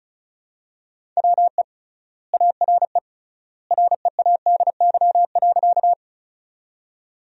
Morse code, heard as a keyed tone, about 36 wpm, 700 Hz. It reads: WE ARE READY.
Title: 3x Word-Spacing - Sentences from Top 400 Words - 35wpm